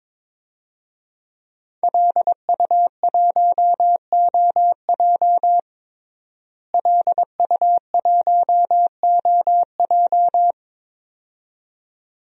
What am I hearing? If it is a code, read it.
Morse code, heard as a keyed tone, about 22 wpm, 700 Hz. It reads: LU1OJ LU1OJ
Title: DXCC-100 Call Signs - 5 Characters - Repeated 2x - 22wpm